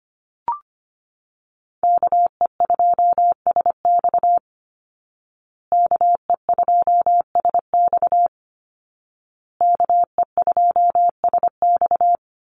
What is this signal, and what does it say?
Morse code, heard as a keyed tone, about 25 wpm, 700 Hz. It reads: KE2SX KE2SX KE2SX
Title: Call Signs Repeated 2x - 25wpm